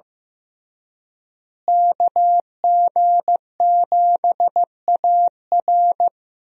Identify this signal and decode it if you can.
Morse code, heard as a keyed tone, about 15 wpm, 700 Hz. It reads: KG7AR
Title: USA Call Signs - 5 Characters - Repeated 3x - 15wpm